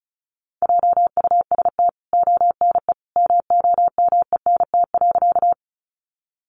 Morse code, heard as a keyed tone, about 34 wpm, 700 Hz. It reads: JUST ONE MOMENT.
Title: Sentences from Top 1000 Words 35wpm